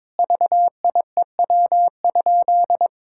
Morse code, heard as a keyed tone, about 22 wpm, 700 Hz. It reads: VIEW?
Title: Sentences from Top 1000 Words - Review Part 6 - 22wpm